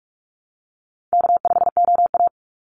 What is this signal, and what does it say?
Morse code, heard as a keyed tone, about 44 wpm, 700 Hz. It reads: K5OA
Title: Easy Call Signs Repeated 3x - 45wpm